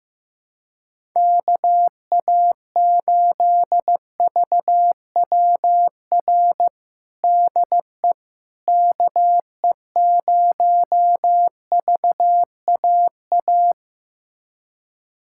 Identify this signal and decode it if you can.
Morse code, heard as a keyed tone, about 15 wpm, 700 Hz. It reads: KA8VWR DE KE0VAA
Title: USA Callsign DE Callsign - 6 Characters - 15wpm